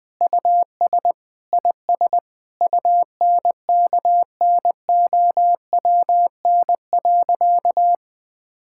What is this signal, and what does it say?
Morse code, heard as a keyed tone, about 20 wpm, 700 Hz. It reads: US IS UNKNOWN.
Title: Sentences with 4-Word Phrases - 20wpm